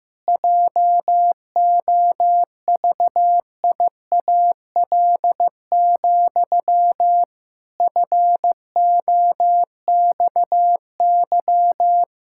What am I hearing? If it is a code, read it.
Morse code, heard as a keyed tone, about 15 wpm, 700 Hz. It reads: JOVIAL, FOXY
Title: Pangram Challenge v2 - 15wpm